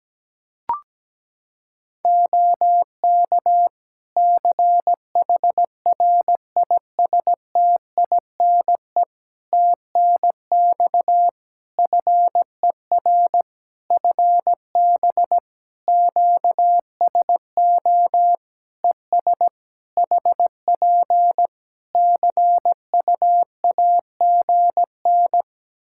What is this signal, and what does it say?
Morse code, heard as a keyed tone, about 17 wpm, 700 Hz. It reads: OK CHRISTINE TNX FER FB QSO ES HP CUAGN
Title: Practice QSO - Exchange 4 - Ending - 17wpm